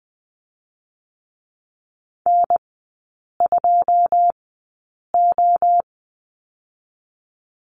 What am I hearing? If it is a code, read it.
Morse code, heard as a keyed tone, about 20 wpm, spaced as for 8 wpm, 700 Hz. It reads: N2O
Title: 081 Lesson 14e - Number 2 - Three Characters 20wpm